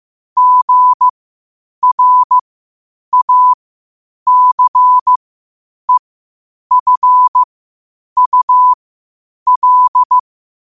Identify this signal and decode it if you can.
Morse code, heard as a keyed tone, about 15 wpm, spaced as for 8 wpm, 1000 Hz. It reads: GRACEFUL
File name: qotdfiles / 8wpm